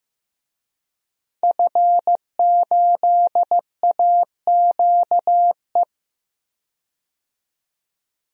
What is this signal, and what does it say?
Morse code, heard as a keyed tone, about 15 wpm, 700 Hz. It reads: F8AQE
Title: DXCC-100 Call Signs - 5 Characters - 15wpm